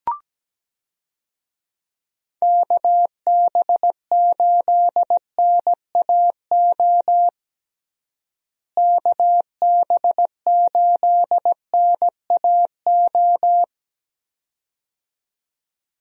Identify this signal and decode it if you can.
Morse code, heard as a keyed tone, about 17 wpm, 700 Hz. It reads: KB8NAO KB8NAO
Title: USA Call Signs - 6 Characters - Repeated 2x - 17wpm